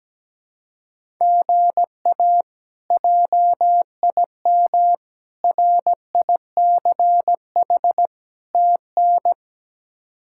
Morse code, heard as a keyed tone, about 17 wpm, 700 Hz. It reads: GA JIM RICH TN
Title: K1USN SST QSO Practice - 17wpm